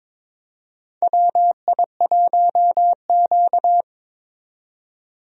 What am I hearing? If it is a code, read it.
Morse code, heard as a keyed tone, about 22 wpm, 700 Hz. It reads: WI1Q
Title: USA Call Signs - 4 Characters - 22wpm